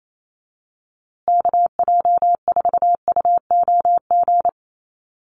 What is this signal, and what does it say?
Morse code, heard as a keyed tone, about 28 wpm, 700 Hz. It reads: KJ4UOG
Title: USA Call Signs Repeated 3x 28wpm